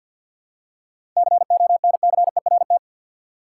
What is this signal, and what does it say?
Morse code, heard as a keyed tone, about 52 wpm, 700 Hz. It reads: CONCERT
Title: Seven Letter Words - 50wpm